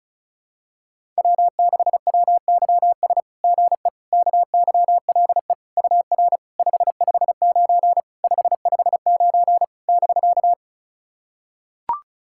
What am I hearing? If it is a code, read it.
Morse code, heard as a keyed tone, about 36 wpm, 690 Hz. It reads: W6WYS GE KYLE UR 559 559 <BK>
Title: SOTA QSO Practice 35wpm